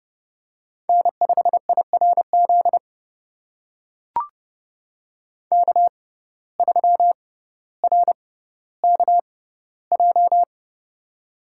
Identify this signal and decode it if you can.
Morse code, heard as a keyed tone, about 30 wpm, spaced as for 15 wpm, 700 Hz. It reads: N5IRZ K3RKJ
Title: USA Call Signs - 5 Characters - 6x Character Spacing - 30wpm